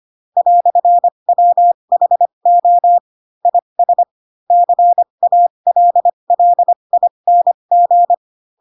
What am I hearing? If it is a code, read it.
Morse code, heard as a keyed tone, about 25 wpm, spaced as for 22 wpm, 700 Hz. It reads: "WHO IS CALLING
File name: Pinocchio / Pinocchio ch4 18 25 ik1hge